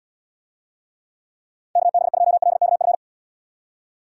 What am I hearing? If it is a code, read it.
Morse code, heard as a keyed tone, about 73 wpm, 700 Hz. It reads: KB2LKF